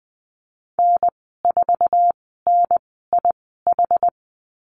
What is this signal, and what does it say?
Morse code, heard as a keyed tone, about 20 wpm, spaced as for 14 wpm, 700 Hz. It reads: N4NIH
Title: 062 Lesson 11g - Letter h - Call Signs 20wpm